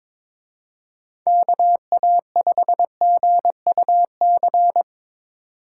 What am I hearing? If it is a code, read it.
Morse code, heard as a keyed tone, about 23 wpm, 700 Hz. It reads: KA5GUC
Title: Speed Racing - USA 2x3 Call Signs 22wpm